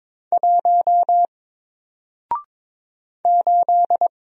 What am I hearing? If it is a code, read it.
Morse code, heard as a keyed tone, about 22 wpm, 700 Hz. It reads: 1 8
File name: Single-Letter-Number-22wpm